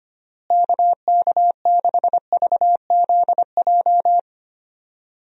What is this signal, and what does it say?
Morse code, heard as a keyed tone, about 25 wpm, 700 Hz. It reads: KK6VZJ
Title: USA Call Signs - 6 Characters - Repeated 2x - 25wpm